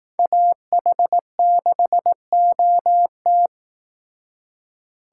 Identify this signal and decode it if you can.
Morse code, heard as a keyed tone, about 18 wpm, 700 Hz. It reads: AH6OT
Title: Speed Racing - USA 2x2 Call Signs 15wpm